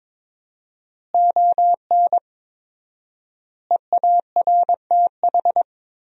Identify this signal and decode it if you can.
Morse code, heard as a keyed tone, about 22 wpm, 700 Hz. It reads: ON EARTH
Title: 4x Word-Spacing - Sentences from Top 200 Words - 22wpm